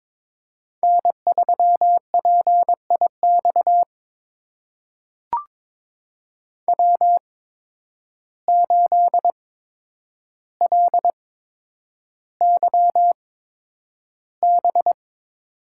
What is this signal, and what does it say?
Morse code, heard as a keyed tone, about 22 wpm, spaced as for 9 wpm, 700 Hz. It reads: N3PIX W8LYB
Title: USA Call Signs - 5 Characters - 8x Character Spacing - 22wpm